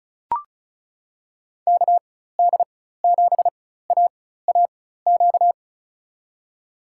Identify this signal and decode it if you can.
Morse code, heard as a keyed tone, about 34 wpm, spaced as for 16 wpm, 700 Hz. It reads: KD7AAQ
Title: USA Call Signs - 6 Characters - 4x Character Spacing - 35wpm